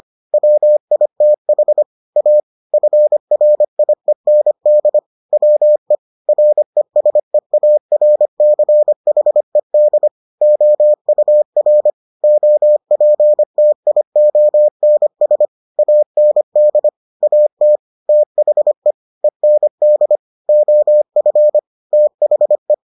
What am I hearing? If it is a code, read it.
Morse code, heard as a keyed tone, about 25 wpm, 600 Hz. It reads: WITH A FRIEND WE RESEARCHED OUR OPTIONS AND AT THE END OF THE